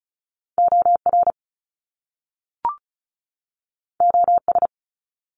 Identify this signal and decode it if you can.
Morse code, heard as a keyed tone, about 36 wpm, 700 Hz. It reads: OR OS
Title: Top Two Letter Combinations in Words 35wpm